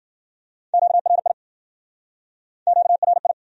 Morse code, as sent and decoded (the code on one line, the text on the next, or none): --.- .-. .. / --.- .-. ..
QRI QRI